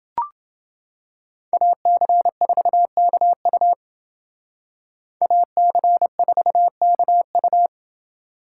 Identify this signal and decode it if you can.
Morse code, heard as a keyed tone, about 28 wpm, 700 Hz. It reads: AC4KU AC4KU